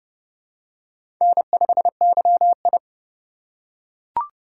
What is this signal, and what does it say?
Morse code, heard as a keyed tone, about 30 wpm, 700 Hz. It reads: N5YI
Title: USA Call Signs - 4 Characters - 2x Character Spacing - 30wpm